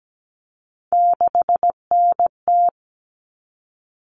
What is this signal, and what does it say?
Morse code, heard as a keyed tone, about 17 wpm, 700 Hz. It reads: 6NT
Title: All Cut Numbers - 3-Digit 17wpm